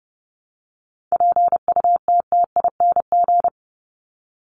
Morse code, heard as a keyed tone, about 30 wpm, 700 Hz. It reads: PUTTING
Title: ING Suffix - Top 100 Words with Base Words 30wpm